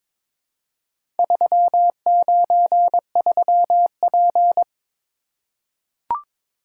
Morse code, heard as a keyed tone, about 22 wpm, 700 Hz. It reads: 393P
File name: practice / 22wpm / Four Letter-Number Combinations - 22wpm